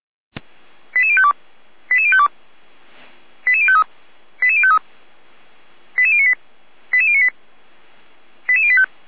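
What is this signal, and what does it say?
Five-tone selective call (ZVEI), heard as quick runs of five tones, tones of about 70 ms. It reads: 88052 88052 88053 88053 88098 88098 88085
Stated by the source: SDR-received ZVEI five-tone call, code 88571.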